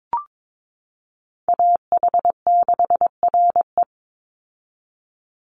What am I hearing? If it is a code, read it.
Morse code, heard as a keyed tone, about 22 wpm, 700 Hz. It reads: AH6RE
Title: Call Signs - 22wpm